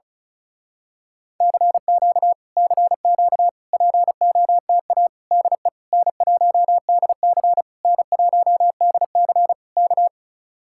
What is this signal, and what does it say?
Morse code, heard as a keyed tone, about 34 wpm, 700 Hz. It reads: CQ CQ POTA DE N1DC N1DC K
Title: Speed-Racing Common QSO Phrases Encore - 35wpm